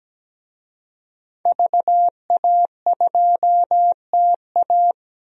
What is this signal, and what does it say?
Morse code, heard as a keyed tone, about 17 wpm, 700 Hz. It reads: VA2TA
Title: DXCC-100 Call Signs - 5 Characters - 17wpm